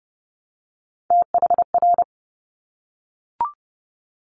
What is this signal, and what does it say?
Morse code, heard as a keyed tone, about 30 wpm, 690 Hz. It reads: THR
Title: Top Three Letter Combinations in Words 30wpm